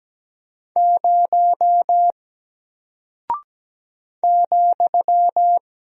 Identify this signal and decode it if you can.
Morse code, heard as a keyed tone, about 17 wpm, 700 Hz. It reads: 0 ,
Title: Single Character (Letter:Number:Punctuation:Prosign) 17wpm